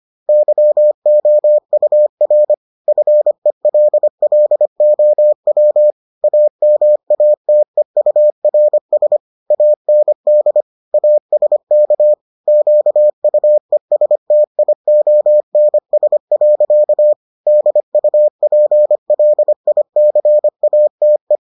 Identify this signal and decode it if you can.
Morse code, heard as a keyed tone, about 25 wpm, 600 Hz. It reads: YOUR FELLOW AMATEURS AND ASK QUESTIONS. DUPLICATE